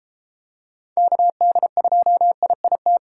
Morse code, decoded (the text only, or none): KD2IIT